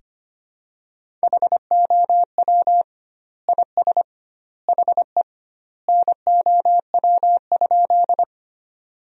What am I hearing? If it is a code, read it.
Morse code, heard as a keyed tone, about 25 wpm, 700 Hz. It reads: HOW IS HE NOW?
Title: 2x Word-Spacing - Sentences from Top 100 Words - 25wpm